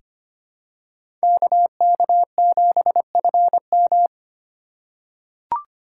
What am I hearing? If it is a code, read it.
Morse code, heard as a keyed tone, about 25 wpm, 700 Hz. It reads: KK7FM
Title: USA Call Signs - 5 Characters - 2x Character Spacing - 25wpm